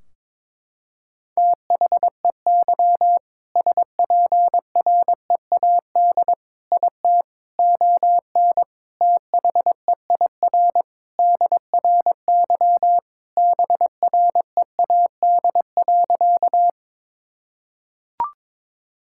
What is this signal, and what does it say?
Morse code, heard as a keyed tone, about 22 wpm, 700 Hz. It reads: THEY SPREAD IT ON THEIR DRY BREAD.
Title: Sentences from Top 1600 Words 22wpm